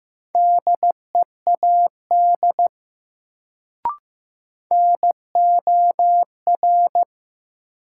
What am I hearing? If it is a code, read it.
Morse code, heard as a keyed tone, about 15 wpm, 700 Hz. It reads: DEAD NOR
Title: Sentences from Top 1500 Words - New Word Drill Alt - 15wpm